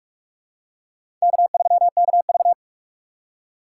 Morse code, heard as a keyed tone, about 41 wpm, 700 Hz. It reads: K3KV
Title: USA Call Signs - 4 Characters - 6x Character Spacing - 45wpm